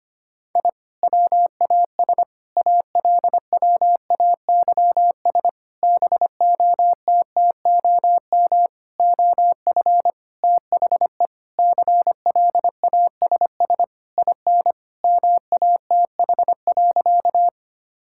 Morse code, heard as a keyed tone, about 25 wpm, 700 Hz. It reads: I WAS ALWAYS BOTTOM OF THE CLASS IN MATH.
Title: Sentences from Top 1000 Words - Review Part 6 - 25wpm